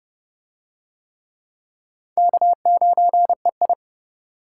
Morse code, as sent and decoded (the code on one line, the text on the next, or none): -.- ----. ..
K9EI